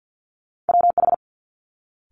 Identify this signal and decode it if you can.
Morse code, heard as a keyed tone, about 50 wpm, 700 Hz. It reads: WH
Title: Multi-letter Phonemes Quick Fire 50wpm